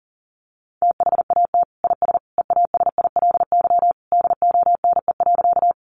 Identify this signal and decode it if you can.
Morse code, heard as a keyed tone, about 40 wpm, 700 Hz. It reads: THAT IS EASILY DONE.
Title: Sentences from Top 900 Words 40wpm